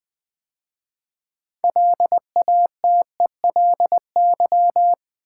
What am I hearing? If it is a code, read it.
Morse code, heard as a keyed tone, about 20 wpm, 700 Hz. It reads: LATELY
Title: 6 Letter Words - First 3 Letters Given - 20wpm